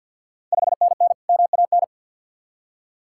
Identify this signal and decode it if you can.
Morse code, heard as a keyed tone, about 52 wpm, 700 Hz. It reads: 5NN MAN
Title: Ontario QSO Party Exchange 50wpm